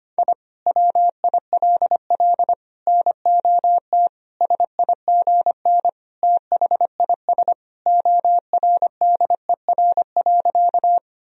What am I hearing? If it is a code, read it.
Morse code, heard as a keyed tone, about 25 wpm, 700 Hz. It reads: I WILL NOT SIGN THIS ORDER.
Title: Sentences from Top 1000 Words - Review Part 2 - 25wpm